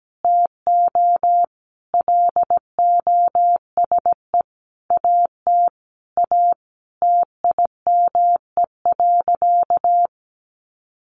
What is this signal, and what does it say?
Morse code, heard as a keyed tone, about 17 wpm, 700 Hz. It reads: TO LOSE AT A TIME.
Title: Sentences from Top 1100 Words 17wpm